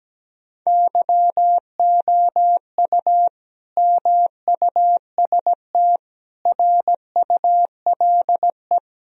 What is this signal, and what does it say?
Morse code, heard as a keyed tone, about 17 wpm, 700 Hz. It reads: YOU MUST RULE
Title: Sentences from Top 1000 Words - Review Part 2 - 17wpm